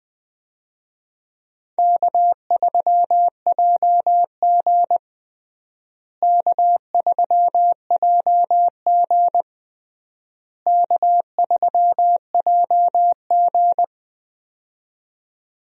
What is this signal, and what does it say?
Morse code, heard as a keyed tone, about 20 wpm, 700 Hz. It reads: K3JG K3JG K3JG
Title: USA Call Signs - 4 Characters - Repeated 3x - 20wpm